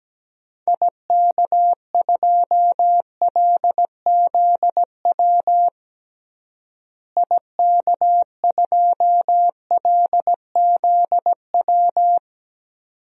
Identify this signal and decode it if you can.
Morse code, heard as a keyed tone, about 17 wpm, 700 Hz. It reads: IK2LZW IK2LZW
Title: DXCC-100 Call Signs - 6 Characters - Repeated 2x - 17wpm